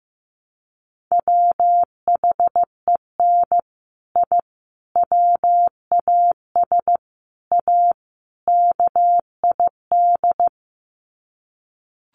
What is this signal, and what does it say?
Morse code, heard as a keyed tone, about 15 wpm, 700 Hz. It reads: WHEN I WAS A KID
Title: Sets of 5 Words 15wpm